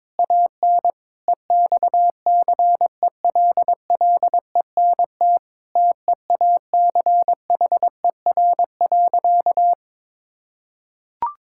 Morse code, heard as a keyed tone, about 22 wpm, 700 Hz. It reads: AN EXCELLENT TEACHER.
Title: Sentences from Top 1000 Words - Review Part 6 - 22wpm